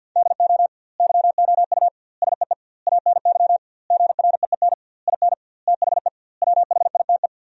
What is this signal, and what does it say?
Morse code, heard as a keyed tone, about 50 wpm, 700 Hz. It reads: DO YOU SEE ANY GREEN IN THE WHITE
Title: 2x Word-Spacing - Sentences from Top 500 Words - 50wpm